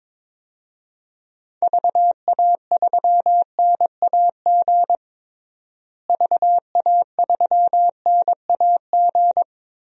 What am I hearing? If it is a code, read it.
Morse code, heard as a keyed tone, about 21 wpm, 700 Hz. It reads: VA3NAG VA3NAG
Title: DXCC-100 Call Signs - 6 Characters - Repeated 3x - 22wpm